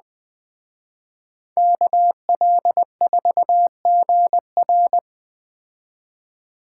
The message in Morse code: KL4GR

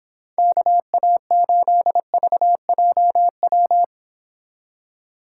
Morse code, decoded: KA8VJW

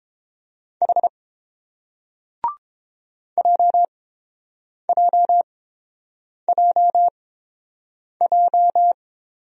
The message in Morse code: H JJJJ

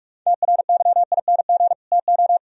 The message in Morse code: TRYING TO

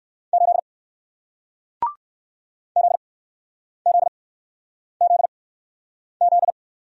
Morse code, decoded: 8 ZZZZ

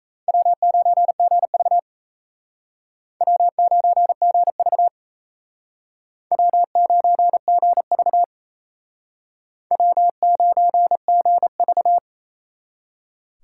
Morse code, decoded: W9GV W9GV W9GV W9GV

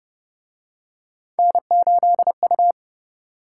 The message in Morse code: N8U

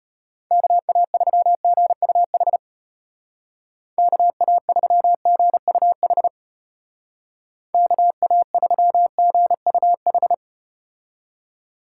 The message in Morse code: KA3GUH KA3GUH KA3GUH

